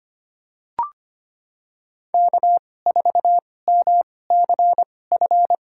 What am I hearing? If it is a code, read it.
Morse code, heard as a keyed tone, about 25 wpm, spaced as for 18 wpm, 700 Hz. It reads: K4MCF
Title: USA Call Signs - 5 Characters - 2x Character Spacing - 25wpm